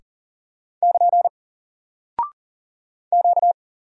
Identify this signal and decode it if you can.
Morse code, heard as a keyed tone, about 40 wpm, 700 Hz. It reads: <KN> Q
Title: Single Character (Letter:Number:Punctuation:Prosign) Alt 40wpm